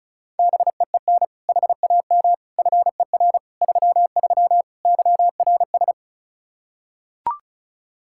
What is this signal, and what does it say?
Morse code, heard as a keyed tone, about 34 wpm, 700 Hz. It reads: BEEN HAM FER 33 YRS
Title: Practice QSO - Exchange 3 - More Info - 35wpm